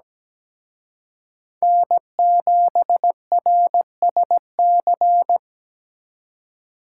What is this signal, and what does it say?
Morse code, heard as a keyed tone, about 20 wpm, 700 Hz. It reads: N7RSC
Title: Speed Racing - USA Call Signs 17wpm